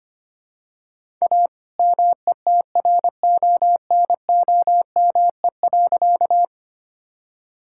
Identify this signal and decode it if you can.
Morse code, heard as a keyed tone, about 25 wpm, 700 Hz. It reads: A METRONOME.
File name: practice / 25wpm / All-Age Appropriate Jokes - 25wpm